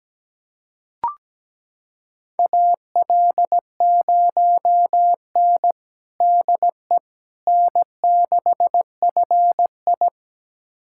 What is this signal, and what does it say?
Morse code, heard as a keyed tone, about 17 wpm, 700 Hz. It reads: AL0N DE N6FI
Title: USA Callsign DE Callsign - 4 Characters - 17wpm